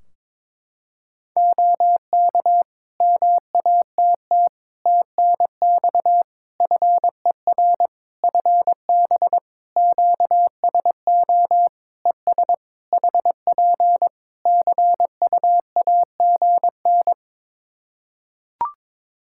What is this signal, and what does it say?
Morse code, heard as a keyed tone, about 22 wpm, 700 Hz. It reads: OK MATT TNX FER FB QSO ES HP CUAGN
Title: Practice QSO - Exchange 4 - Ending - 22wpm